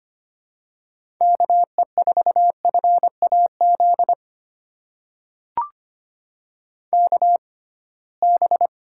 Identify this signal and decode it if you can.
Morse code, heard as a keyed tone, about 25 wpm, spaced as for 19 wpm, 700 Hz. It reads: KE4FAZ KB
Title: USA Call Signs - 6 Characters - 6x Character Spacing - 25wpm